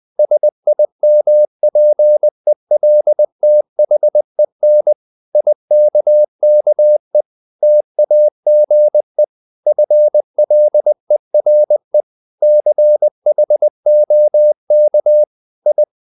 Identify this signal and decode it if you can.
Morse code, heard as a keyed tone, about 20 wpm, 600 Hz. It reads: SIMPELTHEN IKKE TAGE FLERE CHOK I
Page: Ferie 20 wpm